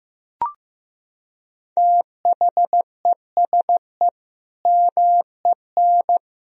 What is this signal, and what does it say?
Morse code, heard as a keyed tone, about 15 wpm, 700 Hz. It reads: THESE MEN
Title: Sentences from Top 1000 Words - Review Part 1 - 15wpm